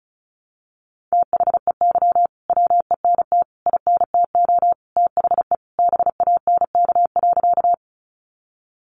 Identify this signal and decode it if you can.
Morse code, heard as a keyed tone, about 34 wpm, 700 Hz. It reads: THEY WENT INTO THE BANK.
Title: Sentences from Top 900 Words 35wpm